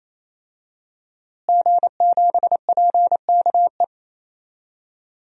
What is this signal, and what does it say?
Morse code, heard as a keyed tone, about 28 wpm, 700 Hz. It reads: G7PKE
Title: DXCC-100 Call Signs - 5 Characters - Repeated 3x - 28wpm